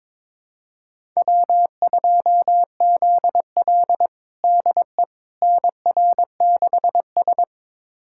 Morse code, heard as a keyed tone, about 22 wpm, 700 Hz. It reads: W2ZL DE NR6S
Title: USA Callsign DE Callsign - 4 Characters - 22wpm